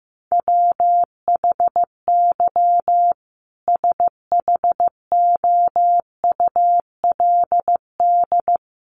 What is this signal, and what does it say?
Morse code, heard as a keyed tone, about 15 wpm, 700 Hz. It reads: WHY SHOULD
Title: Sentences from Top 500 Words 15wpm